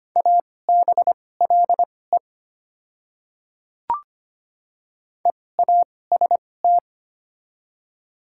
Morse code, heard as a keyed tone, about 25 wpm, spaced as for 18 wpm, 700 Hz. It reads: ABLE EAST
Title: 2x Character Spacing - Top 500 Words - 25wpm